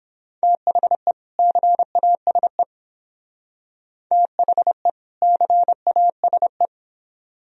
Morse code, heard as a keyed tone, about 26 wpm, 700 Hz. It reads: THE CASE THE CASE
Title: Speed-Racing - Sets of 2 Words 22wpm